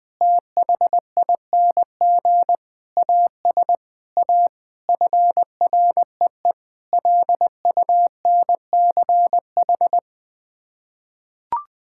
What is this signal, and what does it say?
Morse code, heard as a keyed tone, about 20 wpm, 700 Hz. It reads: THING AS A FREE LUNCH
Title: Sets of 9 Words - 20wpm